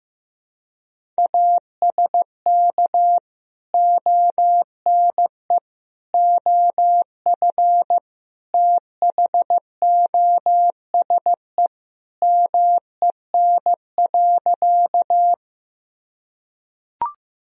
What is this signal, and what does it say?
Morse code, heard as a keyed tone, about 15 wpm, 700 Hz. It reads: ASK ONE OF THOSE MEN.
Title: Sentences from Top 1000 Words - Review Part 1 - 15wpm